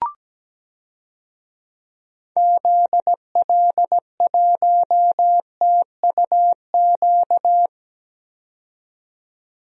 Morse code, heard as a keyed tone, about 17 wpm, 700 Hz. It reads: ZL1TUQ
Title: DXCC-100 Call Signs - 6 Characters - 17wpm